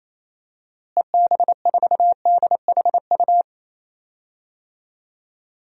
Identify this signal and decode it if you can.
Morse code, heard as a keyed tone, about 28 wpm, 700 Hz. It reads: EB4DHU
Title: DXCC-100 Call Signs - 6 Characters - Repeated 2x - 28wpm